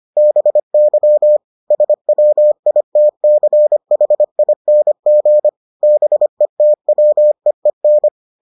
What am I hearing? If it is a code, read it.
Morse code, heard as a keyed tone, about 25 wpm, 600 Hz. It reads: BY SWITCHING BETWEEN